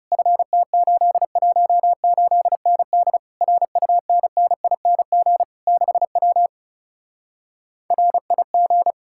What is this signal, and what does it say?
Morse code, heard as a keyed tone, about 40 wpm, 700 Hz. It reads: FT818ND RUNNING 6W RIG